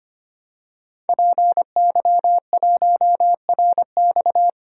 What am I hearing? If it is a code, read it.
Morse code, heard as a keyed tone, about 25 wpm, 700 Hz. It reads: PY1RX
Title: DXCC-100 Call Signs - 5 Characters - 25wpm